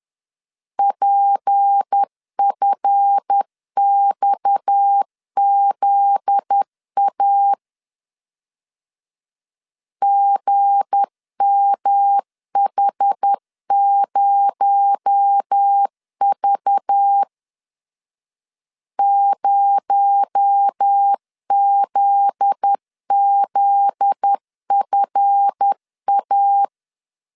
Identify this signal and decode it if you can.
Morse code, heard as a keyed tone, about 10 wpm, 800 Hz. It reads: PFXZA GMH0V 0ZZFA
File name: LEZIONI / 7 GF9Z / 7 GF9Z 10wpm 2611